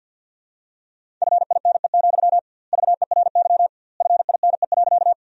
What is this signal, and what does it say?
Morse code, heard as a keyed tone, about 51 wpm, 700 Hz. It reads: FINE, VERY FINE.
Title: 2x Word-Spacing - Sentences from Top 500 Words - 50wpm